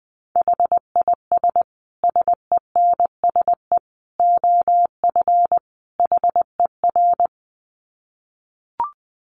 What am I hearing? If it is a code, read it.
Morse code, heard as a keyed tone, about 20 wpm, 700 Hz. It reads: HIS SENSE OF HER
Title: Dit Challenge 20wpm